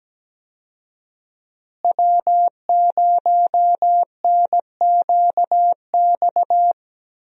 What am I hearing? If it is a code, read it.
Morse code, heard as a keyed tone, about 17 wpm, 700 Hz. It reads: W0NQX
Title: USA Call Signs - 5 Characters - Repeated 2x - 17wpm